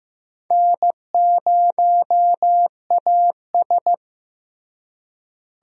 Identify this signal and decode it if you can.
Morse code, heard as a keyed tone, about 15 wpm, 700 Hz. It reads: N0AS
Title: USA Call Signs - 4 Characters - Repeated 2x - 15wpm